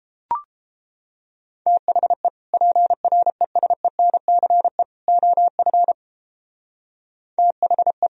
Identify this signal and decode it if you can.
Morse code, heard as a keyed tone, about 31 wpm, 700 Hz. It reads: THE PRESENCE OF THE